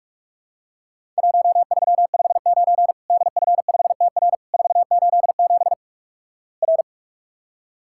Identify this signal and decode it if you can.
Morse code, heard as a keyed tone, about 46 wpm, 700 Hz. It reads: 1359 DF5TR 487 R
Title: Worked All Europe DX Contest - QTC Exchange - 45wpm